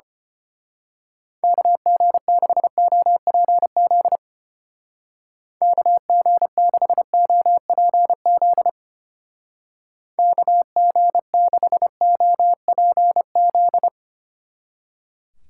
Morse code, heard as a keyed tone, about 29 wpm, 700 Hz. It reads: KG6OPZ KG6OPZ KG6OPZ
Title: Speed Racing - USA 2x3 Call Signs 25wpm